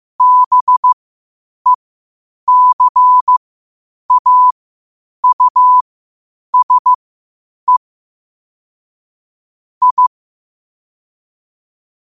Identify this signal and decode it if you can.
Morse code, heard as a keyed tone, about 15 wpm, spaced as for 8 wpm, 1000 Hz. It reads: BECAUSE I
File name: qotdfiles / 8wpm